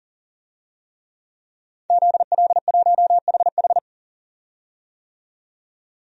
Morse code, heard as a keyed tone, about 40 wpm, 700 Hz. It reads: ZL1HH
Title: DXCC-100 Call Signs - 5 Characters - 40wpm